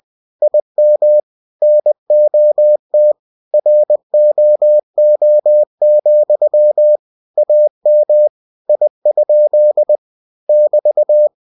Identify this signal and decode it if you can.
Morse code, heard as a keyed tone, about 20 wpm, 600 Hz. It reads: IM NOT ROO, AM I? <BT>
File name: cw-books / winnie-the-pooh / winnie-20wpm-page-0073